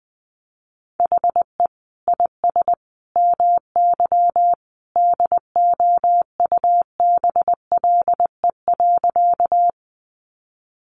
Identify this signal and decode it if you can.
Morse code, heard as a keyed tone, about 19 wpm, 700 Hz. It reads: HE IS MY DOUBLE.
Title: Sentences from Top 1400 Words 20wpm